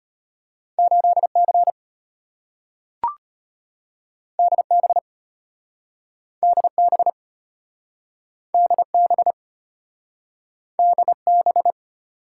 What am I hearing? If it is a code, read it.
Morse code, heard as a keyed tone, about 33 wpm, 690 Hz. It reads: 8C DB DB DB DB